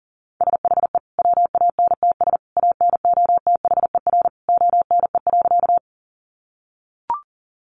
Morse code, heard as a keyed tone, about 37 wpm, 700 Hz. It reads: SHE WANTS ANOTHER ONE.
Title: Easy Phrases 40wpm